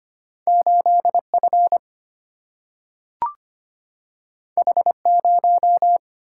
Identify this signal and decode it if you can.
Morse code, heard as a keyed tone, about 25 wpm, 700 Hz. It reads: 8 F H 0
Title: Single-Letter Single-Number 25wpm